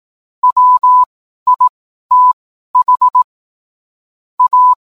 Morse code, heard as a keyed tone, about 18 wpm, spaced as for 12 wpm, 1000 Hz. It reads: WITH A
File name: cwblog / cw mp3 directory / mem morse 12